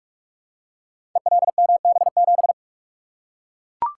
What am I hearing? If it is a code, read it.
Morse code, heard as a keyed tone, about 43 wpm, 700 Hz. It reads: ELMB7
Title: Code Groups - 5 Letter:Numbers - 45wpm